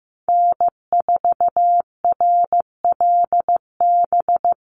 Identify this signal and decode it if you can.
Morse code, heard as a keyed tone, about 15 wpm, 700 Hz. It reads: N4RLB
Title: USA Call Signs Repeated 2x 15wpm